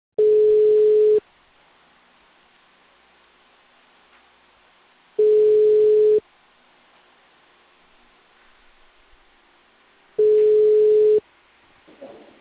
Ringback tone (Europe, 425 Hz).